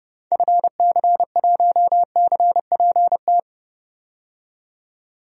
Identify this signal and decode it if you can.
Morse code, heard as a keyed tone, about 30 wpm, 700 Hz. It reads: FC1CPT
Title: DXCC-100 Call Signs - 6 Characters - 30wpm